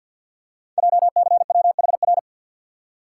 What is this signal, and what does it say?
Morse code, heard as a keyed tone, about 48 wpm, 700 Hz. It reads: JCWHR
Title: Code Groups - 5 Letters - 50wpm